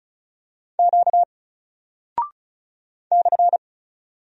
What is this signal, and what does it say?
Morse code, heard as a keyed tone, about 35 wpm, 700 Hz. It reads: Q /
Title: Single Character (Letter:Number:Punctuation:Prosign) 35wpm